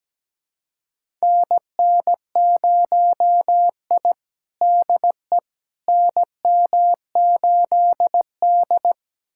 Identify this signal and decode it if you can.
Morse code, heard as a keyed tone, about 17 wpm, 700 Hz. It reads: NN0I DE NM8D
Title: USA Callsign DE Callsign - 4 Characters - 17wpm